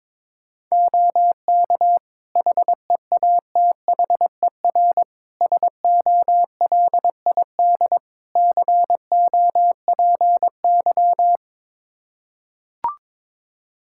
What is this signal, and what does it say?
Morse code, heard as a keyed tone, about 22 wpm, 700 Hz. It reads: OK HEATHER SOLID COPY
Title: K1USN SST QSO Practice 22wpm